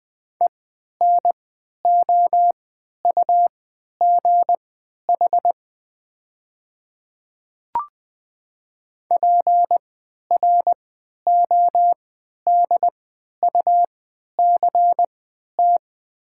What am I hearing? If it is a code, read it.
Morse code, heard as a keyed tone, about 20 wpm, spaced as for 11 wpm, 700 Hz. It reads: ENOUGH PRODUCT
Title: 3x Character Spacing - Top 500 Words - 20wpm